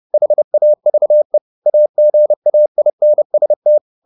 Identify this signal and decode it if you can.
Morse code, heard as a keyed tone, about 30 wpm, 600 Hz. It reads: HAVE AGAINST
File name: the old man and the sea / the old man and the sea 1 30wpm